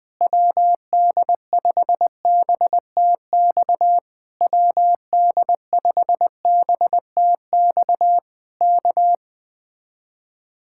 Morse code, heard as a keyed tone, about 20 wpm, 700 Hz. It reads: WD5BTX WD5BTX K
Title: Speed-Racing Common QSO Phrases - 20wpm